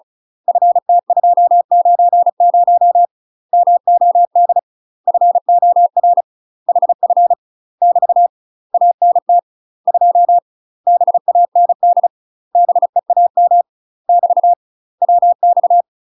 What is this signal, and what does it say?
Morse code, heard as a keyed tone, about 35 wpm, 700 Hz. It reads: FT290 MOD FOR HF <BT> ANT 2 BAND BEAM <BT> WX
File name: UK MASTER QSOs TRAINING/35WPM